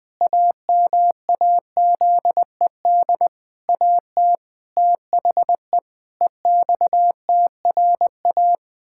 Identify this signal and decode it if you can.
Morse code, heard as a keyed tone, about 20 wpm, 700 Hz. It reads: AMAZED AT THE EXTRA
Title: Pangram Challenge v2 - 20wpm